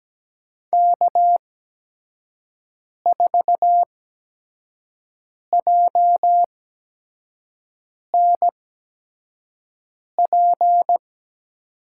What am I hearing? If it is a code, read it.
Morse code, heard as a keyed tone, about 17 wpm, spaced as for 5 wpm, 700 Hz. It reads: K4JNP
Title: USA Call Signs - 5 Characters - 8x Character Spacing - 17wpm